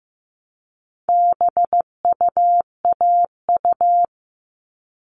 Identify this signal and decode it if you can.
Morse code, heard as a keyed tone, about 15 wpm, 700 Hz. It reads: BUAU
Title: All Cut Numbers - 4-Digit 15wpm